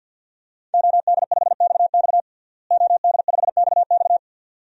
Morse code, heard as a keyed tone, about 49 wpm, 700 Hz. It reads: OD5XX OD5XX
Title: DXCC-100 Call Signs - 5 Characters - Repeated 2x - 50wpm